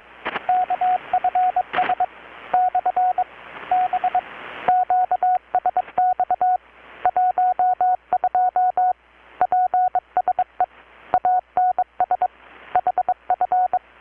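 Morse code, heard as a keyed tone, about 22 wpm, 720 Hz, with radio noise. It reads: KFS / B QSX 12 PSE ANS HF
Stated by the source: Morse from KPH's Night of Nights XX maritime radiotelegraphy event, received in Chiapas, Mexico about 3,800 km away on a Kenwood R-600 with a grounded 10 m long-wire antenna.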